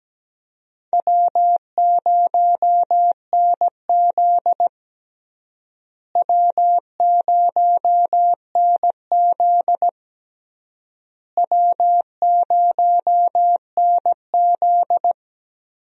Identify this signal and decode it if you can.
Morse code, heard as a keyed tone, about 17 wpm, 700 Hz. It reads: W0NZ W0NZ W0NZ
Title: USA Call Signs - 4 Characters - Repeated 3x - 17wpm